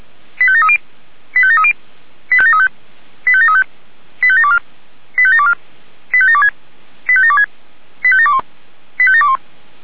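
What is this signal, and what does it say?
Five-tone selective call (ZVEI), heard as quick runs of five tones, tones of about 70 ms. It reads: 85730 85730 85735 85735 85724 85724 85726 85726 85721 85721
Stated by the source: ZVEI five-tone selective call 85722 received via SDR.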